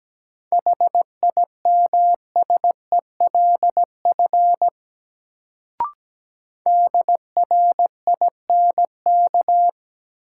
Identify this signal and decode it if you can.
Morse code, heard as a keyed tone, about 17 wpm, 700 Hz. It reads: HIMSELF DRINK
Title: Sentences from Top 1500 Words - New Word Drill Alt - 17wpm